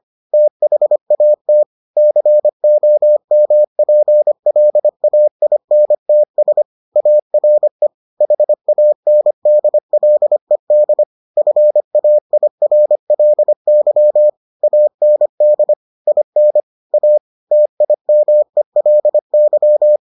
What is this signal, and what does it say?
Morse code, heard as a keyed tone, about 25 wpm, 600 Hz. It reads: THAT COMPLAINTS ARE HANDLED FAIRLY AND IN A TIMELY